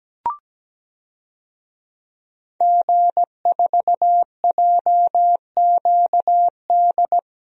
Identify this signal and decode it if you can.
Morse code, heard as a keyed tone, about 17 wpm, 700 Hz. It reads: G4JQD
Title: DXCC-100 Call Signs - 5 Characters - 17wpm